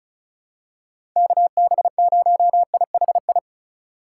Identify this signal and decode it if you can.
Morse code, heard as a keyed tone, about 34 wpm, 700 Hz. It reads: KB0IHI
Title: USA Call Signs - 6 Characters - 2x Character Spacing - 35wpm